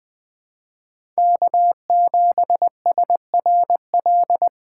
Morse code, heard as a keyed tone, about 20 wpm, 700 Hz. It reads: K7SRL
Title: USA Call Signs - 5 Characters - 2x Character Spacing - 20wpm